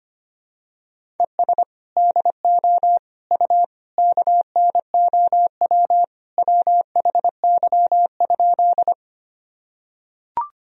Morse code, heard as a keyed tone, about 25 wpm, 700 Hz. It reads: ES DO U KNOW WHY?
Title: Sentences from Top 200 Words and Abbreviations 25wpm